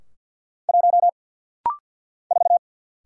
Morse code, heard as a keyed tone, about 50 wpm, 700 Hz. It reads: 1 4
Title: Single Letter-Number - Rapid-Fire 50wpm